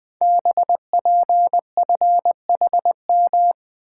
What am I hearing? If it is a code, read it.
Morse code, heard as a keyed tone, about 20 wpm, 700 Hz. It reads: BPFHM
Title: Code Groups - 5 Letters - 20wpm